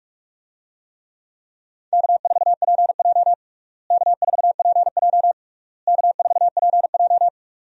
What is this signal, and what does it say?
Morse code, heard as a keyed tone, about 46 wpm, 700 Hz. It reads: K4PJ K4PJ K4PJ
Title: USA Call Signs - 4 Characters - Repeated 3x - 45wpm